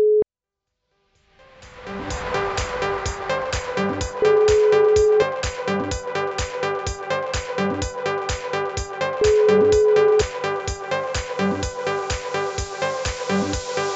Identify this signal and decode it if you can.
Ringback tone (Europe, 425 Hz), with noise.